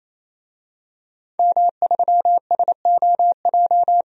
Morse code, heard as a keyed tone, about 28 wpm, 700 Hz. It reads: M3SOJ
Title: DXCC-100 Call Signs - 5 Characters - 28wpm